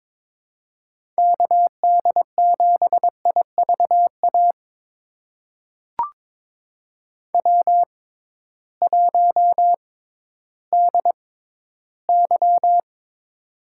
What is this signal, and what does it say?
Morse code, heard as a keyed tone, about 22 wpm, spaced as for 13 wpm, 700 Hz. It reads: KD7IVA W1DY
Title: USA Call Signs - 6x Character Spacing - 22wpm